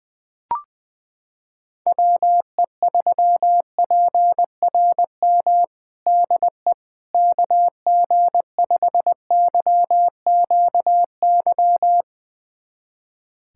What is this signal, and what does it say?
Morse code, heard as a keyed tone, about 21 wpm, 700 Hz. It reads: WE3PRM DE KG5YQY
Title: USA Callsign DE Callsign - 6 Characters - 20wpm